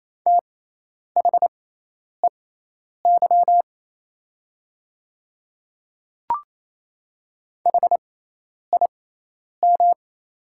Morse code, heard as a keyed tone, about 28 wpm, spaced as for 10 wpm, 690 Hz. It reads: THEY HIM
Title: 6x Character Spacing - Top 200 Words - 28wpm